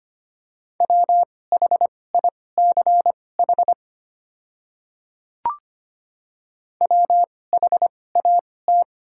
Morse code, heard as a keyed tone, about 25 wpm, spaced as for 18 wpm, 690 Hz. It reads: WHICH WHAT
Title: 2x Character Spacing - Top 100 Words - 25wpm